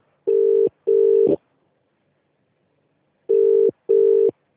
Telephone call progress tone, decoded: ringback tone